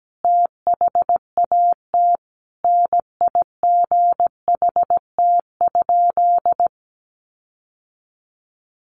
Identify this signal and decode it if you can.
Morse code, heard as a keyed tone, about 17 wpm, 700 Hz. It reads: THAT NIGHT?
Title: Sentences from Top 1300 Words 17wpm